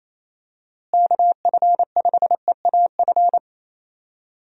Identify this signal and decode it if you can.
Morse code, heard as a keyed tone, about 28 wpm, 700 Hz. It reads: KF5EAF
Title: USA Call Signs - 6 Characters - Repeated 2x - 28wpm